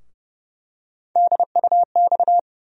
Morse code, heard as a keyed tone, about 30 wpm, 700 Hz. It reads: DUX
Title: USA Call Sign Postfixes - 3 Letters 30wpm